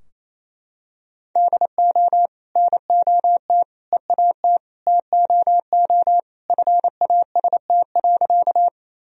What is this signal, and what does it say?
Morse code, heard as a keyed tone, about 28 wpm, 700 Hz. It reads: DO NOT EAT TOO FAST.
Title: Sentences from Top 1000 Words - Review Part 2 - 28wpm